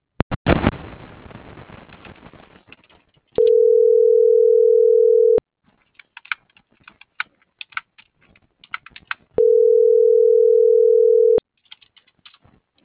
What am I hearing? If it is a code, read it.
Ringback tone (North America), 440 + 480 Hz.
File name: CENTRAL MABE / out sebastian.espinosa